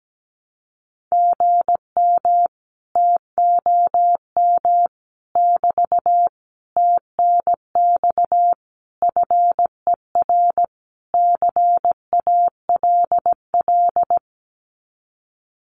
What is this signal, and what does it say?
Morse code, heard as a keyed tone, about 17 wpm, 700 Hz. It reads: GM TOM <BT> TNX FER CALL
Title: Common QSO Phrases Encore 17wpm